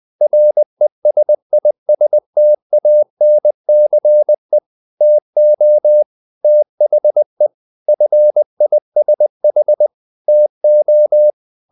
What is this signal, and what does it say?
Morse code, heard as a keyed tone, about 20 wpm, 600 Hz. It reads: RESISTANCE TO THE FISH TO
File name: the old man and the sea part 2 20wpm